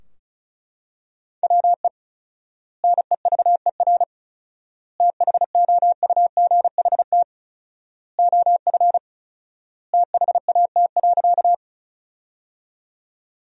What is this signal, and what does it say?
Morse code, heard as a keyed tone, about 36 wpm, 700 Hz. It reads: WE NEVER THOUGHT OF THAT.
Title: 4x Word-Spacing - Sentences from Top 300 Words - 35wpm